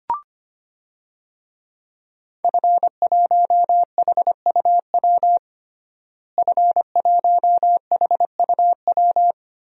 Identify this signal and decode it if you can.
Morse code, heard as a keyed tone, about 25 wpm, 700 Hz. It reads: F1HUW F1HUW
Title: DXCC-100 Call Signs - 5 Characters - Repeated 3x - 25wpm